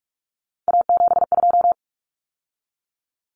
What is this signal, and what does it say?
Morse code, heard as a keyed tone, about 44 wpm, 700 Hz. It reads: A72
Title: Common Cut Numbers - 3-Digit 45wpm